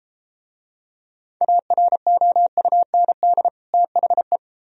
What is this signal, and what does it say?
Morse code, heard as a keyed tone, about 33 wpm, 700 Hz. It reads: AROUND THE